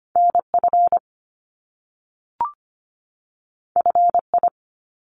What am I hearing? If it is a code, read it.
Morse code, heard as a keyed tone, about 25 wpm, 700 Hz. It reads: N F F I
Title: Top Two Letter Combinations in Words 25wpm